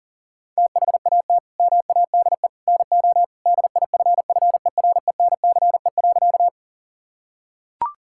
Morse code, heard as a keyed tone, about 40 wpm, 700 Hz. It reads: THAT MADE NO DIFFERENCE.
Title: Sentences from Top 1000 Words - Review Part 5 - 40wpm